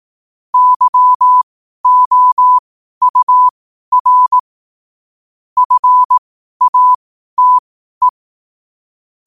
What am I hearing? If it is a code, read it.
Morse code, heard as a keyed tone, about 18 wpm, spaced as for 12 wpm, 1000 Hz. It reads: YOUR FATE
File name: qotdfiles / 12wpm